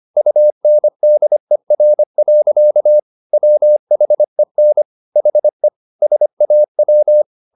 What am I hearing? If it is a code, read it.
Morse code, heard as a keyed tone, about 25 wpm, 600 Hz. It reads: UNDER. WHEN HE SAW